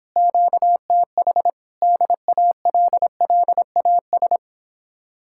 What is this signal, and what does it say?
Morse code, heard as a keyed tone, about 31 wpm, 700 Hz. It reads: QTH DALLAS